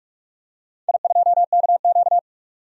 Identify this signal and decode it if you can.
Morse code, heard as a keyed tone, about 44 wpm, 700 Hz. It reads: I2KQ
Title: DXCC-100 Call Signs - 4 Characters - 45wpm